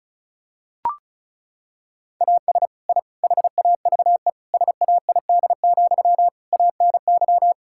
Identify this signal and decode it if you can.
Morse code, heard as a keyed tone, about 34 wpm, 700 Hz. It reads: AS I HAVE SAID, ANY
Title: Sentences from Top 1000 Words - Review Part 6 - 35wpm